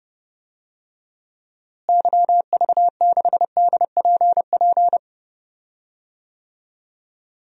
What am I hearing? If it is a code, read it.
Morse code, heard as a keyed tone, about 30 wpm, 700 Hz. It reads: YV6DPP
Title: DXCC-100 Call Signs - 6 Characters - Repeated 2x - 30wpm